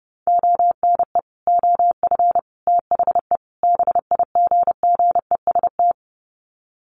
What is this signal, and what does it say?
Morse code, heard as a keyed tone, about 30 wpm, 700 Hz. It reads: ONE OF THE BIGGEST
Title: Sets of 4 Words 30wpm